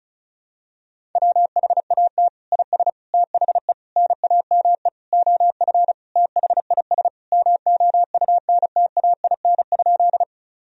Morse code, heard as a keyed tone, about 34 wpm, 700 Hz. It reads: WHAT IS THE NAME OF THIS MOUNTAIN?
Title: Sentences from Top 1000 Words - Review Part 1 - 35wpm